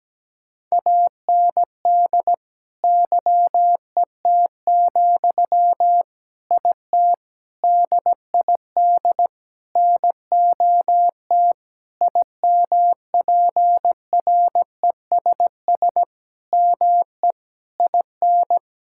Morse code, heard as a keyed tone, about 17 wpm, 700 Hz. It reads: AND YET, IT DID NOT IMPRESS ME IN
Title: Sentences from Top 1600 Words 17wpm